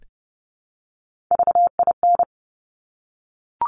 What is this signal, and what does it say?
Morse code, heard as a keyed tone, about 30 wpm, 700 Hz. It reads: VIN
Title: Top Three Letter Combinations in Words 30wpm